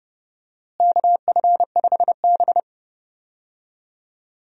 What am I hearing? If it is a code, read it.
Morse code, heard as a keyed tone, about 31 wpm, 700 Hz. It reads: KF5B
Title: USA Call Signs - 4 Characters - Repeated 2x - 30wpm